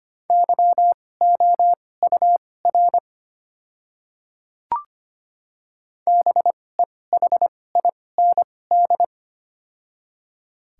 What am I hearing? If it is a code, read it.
Morse code, heard as a keyed tone, about 25 wpm, spaced as for 18 wpm, 700 Hz. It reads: YOUR BEHIND
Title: 2x Character Spacing - Top 500 Words - 25wpm